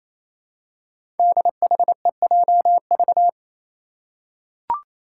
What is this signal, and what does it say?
Morse code, heard as a keyed tone, about 27 wpm, 700 Hz. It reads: DHEJV
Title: Five Challenging Letter-Number Combinations 28wpm